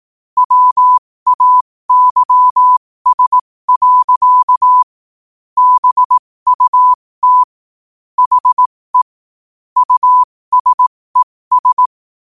Morse code, heard as a keyed tone, about 18 wpm, spaced as for 15 wpm, 1000 Hz. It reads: WAYS. BUT HE USES
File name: qotdfiles / 15wpm